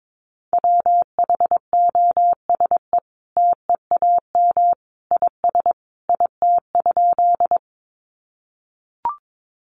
Morse code, heard as a keyed tone, about 22 wpm, 700 Hz. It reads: WHOSE TEAM IS IT?
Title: Sentences from Top 1500 Words 22wpm